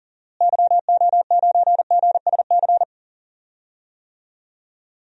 Morse code, heard as a keyed tone, about 41 wpm, 700 Hz. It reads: YO9GSC
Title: DXCC-100 Call Signs - 6 Characters - Repeated 3x - 40wpm